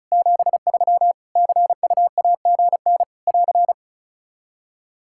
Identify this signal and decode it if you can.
Morse code, heard as a keyed tone, about 34 wpm, 700 Hz. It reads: 73 CUAGN <AR>
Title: Practice QSO - Exchange 4 - Ending - 35wpm